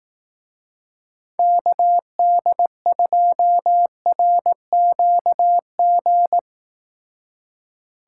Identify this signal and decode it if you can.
Morse code, heard as a keyed tone, about 19 wpm, 700 Hz. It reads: KD2RQG